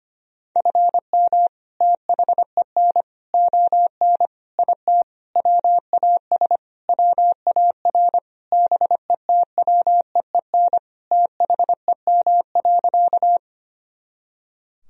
Morse code, heard as a keyed tone, about 25 wpm, 700 Hz. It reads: FM THEN ON IT WAS WAR BETWEEN THEM.
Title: Sentences from Top 400 Words and Abbreviations 25wpm